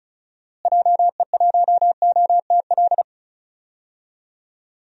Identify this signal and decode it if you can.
Morse code, heard as a keyed tone, about 34 wpm, 700 Hz. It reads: JE1OTL
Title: DXCC-100 Call Signs - 6 Characters - Repeated 2x - 35wpm